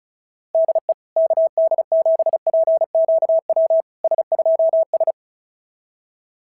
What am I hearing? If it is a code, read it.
Morse code, heard as a keyed tone, about 36 wpm, 690 Hz. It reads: DE KD7PQW S2S
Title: SOTA QSO Practice 35wpm